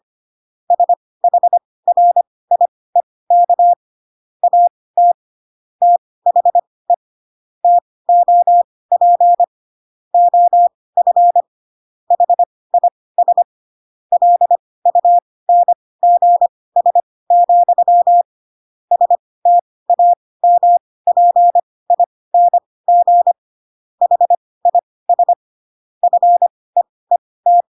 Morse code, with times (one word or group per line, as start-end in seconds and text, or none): 0.7-3.7 SHRIEK
4.4-5.1 AT
5.8-6.9 THE
7.6-9.4 TOP
10.1-11.4 OF
12.1-13.4 HIS
14.1-18.2 LUNGS,
18.9-23.3 STAMPING
24.0-25.3 HIS
26.0-27.6 FEET